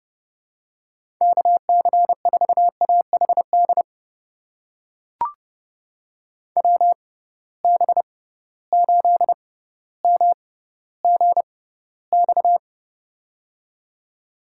Morse code, heard as a keyed tone, about 30 wpm, spaced as for 15 wpm, 700 Hz. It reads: KC4AHD WB8MGX
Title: USA Call Signs - 6 Characters - 6x Character Spacing - 30wpm